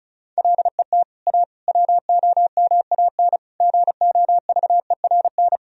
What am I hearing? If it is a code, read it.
Morse code, heard as a keyed tone, about 36 wpm, 700 Hz. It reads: LET A WOMAN GOVERN
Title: Sentences from Top 1000 Words - Review Part 2 - 35wpm